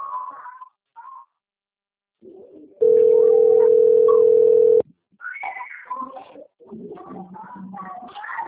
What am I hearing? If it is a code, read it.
Ringback tone (North America), 440 + 480 Hz.
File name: out- - - - - .3001